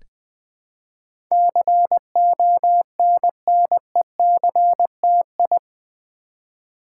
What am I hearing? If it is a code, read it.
Morse code, heard as a keyed tone, about 20 wpm, 700 Hz. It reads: CONNECTI
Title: Build A Word - 4 Letters At A Time - 10-Letter Words - 20wpm